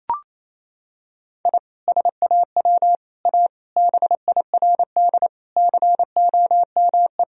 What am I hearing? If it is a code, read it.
Morse code, heard as a keyed tone, about 27 wpm, 700 Hz. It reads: I SAW A BIRD COME
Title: Sentences from Top 1000 Words - Review Part 2 - 28wpm